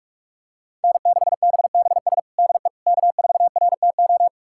Morse code, heard as a keyed tone, about 46 wpm, 700 Hz. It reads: N6BBS DE K4RTO
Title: USA Callsign DE Callsign - 5 Characters - 45wpm